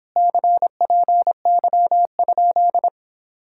Morse code, heard as a keyed tone, about 26 wpm, 700 Hz. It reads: CPY?